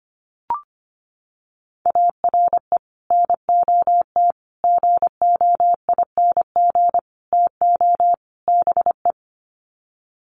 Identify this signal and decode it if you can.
Morse code, heard as a keyed tone, about 25 wpm, 700 Hz. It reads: ARE NOT GOING TO BE
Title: Sets of 5 Words Encore 25wpm